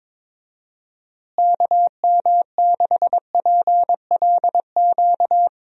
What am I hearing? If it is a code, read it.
Morse code, heard as a keyed tone, about 22 wpm, 700 Hz. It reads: KM6PLQ
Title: USA Call Signs - 6 Characters - Repeated 3x - 22wpm